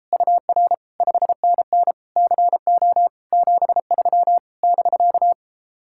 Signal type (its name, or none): Morse code